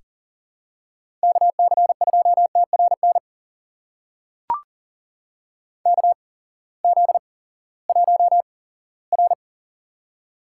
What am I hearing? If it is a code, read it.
Morse code, heard as a keyed tone, about 40 wpm, spaced as for 20 wpm, 700 Hz. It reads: KC2TRN KZ1R
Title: USA Call Signs - 8x Character Spacing - 40wpm